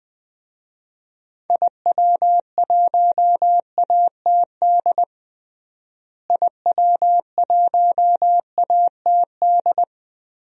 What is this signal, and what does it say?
Morse code, heard as a keyed tone, about 20 wpm, 700 Hz. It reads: IW1ATD IW1ATD
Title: DXCC-100 Call Signs - 6 Characters - Repeated 3x - 20wpm